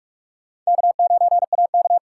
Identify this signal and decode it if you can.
Morse code, heard as a keyed tone, about 45 wpm, 700 Hz. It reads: K9AK
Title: USA Call Signs - 4 Characters - 4x Character Spacing - 45wpm